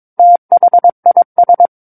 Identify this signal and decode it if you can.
Morse code, heard as a keyed tone, about 22 wpm, 700 Hz. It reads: THIS
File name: Book One / 13-Chapter 13 - 22 WPM